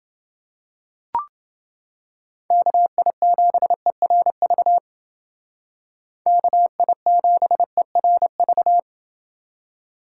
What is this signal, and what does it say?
Morse code, heard as a keyed tone, about 28 wpm, 700 Hz. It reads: KI7ERV KI7ERV